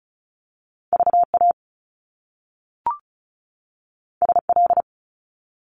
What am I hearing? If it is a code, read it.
Morse code, heard as a keyed tone, about 36 wpm, 700 Hz. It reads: VA SL
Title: Top Two Letter Combinations in Words 35wpm